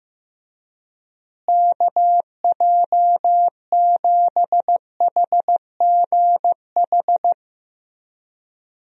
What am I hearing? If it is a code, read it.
Morse code, heard as a keyed tone, about 15 wpm, 700 Hz. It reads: KJ7HGH
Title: USA Call Signs - 6 Characters - Repeated 2x - 15wpm